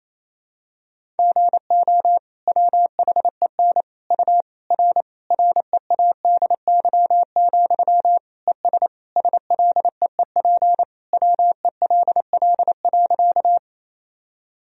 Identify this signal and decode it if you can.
Morse code, heard as a keyed tone, about 28 wpm, 700 Hz. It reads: GO WHEN U R READY, ES SLEEP WELL.
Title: Sentences from Top 500 Words and Abbreviations 28wpm